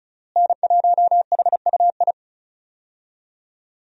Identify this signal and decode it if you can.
Morse code, heard as a keyed tone, about 36 wpm, 700 Hz. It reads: N1HUI
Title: USA Call Signs - 5 Characters - Repeated 3x - 35wpm